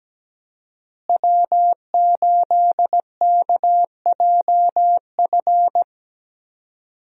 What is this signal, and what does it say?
Morse code, heard as a keyed tone, about 17 wpm, 700 Hz. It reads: W8KJF
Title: USA Call Signs - 5 Characters - Repeated 3x - 17wpm